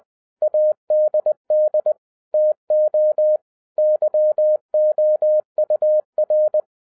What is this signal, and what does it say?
Morse code, heard as a keyed tone, about 20 wpm, 610 Hz. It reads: ADD TO YOUR
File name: 2011-07-mp3 / n43 Czech Army Morse Keys 20wpm